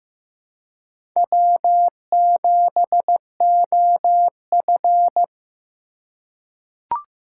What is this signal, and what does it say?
Morse code, heard as a keyed tone, about 15 wpm, 700 Hz. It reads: W7OF
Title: USA Call Signs - 4 Characters - 4x Character Spacing - 15wpm